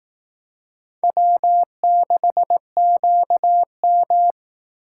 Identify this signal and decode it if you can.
Morse code, heard as a keyed tone, about 18 wpm, 700 Hz. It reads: W6QM
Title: Speed Racing - USA 1x2 Call Signs 15wpm